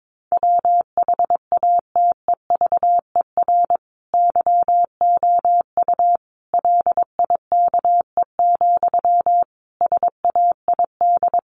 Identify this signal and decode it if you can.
Morse code, heard as a keyed tone, about 22 wpm, 700 Hz. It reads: WHATEVER YOU LIKE, SAID
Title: Sentences from Top 1100 Words 22wpm